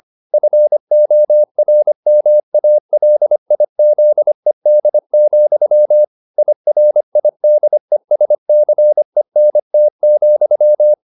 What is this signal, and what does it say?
Morse code, heard as a keyed tone, about 25 wpm, 600 Hz. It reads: FORMALIZED, IRIDESCENT,
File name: the old man and the sea part 1 25wpm